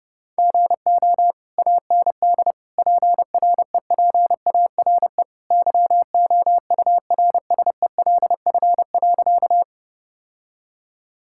Morse code, heard as a keyed tone, about 32 wpm, 700 Hz. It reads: GO AND PREPARE YOURSELF.
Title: Sentences from Top 1000 Words - Review Part 6 - 30wpm